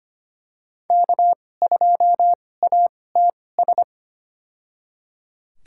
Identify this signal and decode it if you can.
Morse code, heard as a keyed tone, about 25 wpm, spaced as for 18 wpm, 700 Hz. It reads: K2ATS
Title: USA Call Signs - 5 Characters - 2x Character Spacing - 25wpm